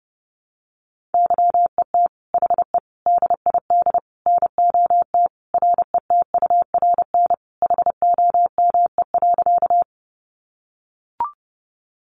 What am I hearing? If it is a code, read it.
Morse code, heard as a keyed tone, about 28 wpm, 700 Hz. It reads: YET HE DID NOT RETURN HOME.
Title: Sentences from Top 800 Words 30wpm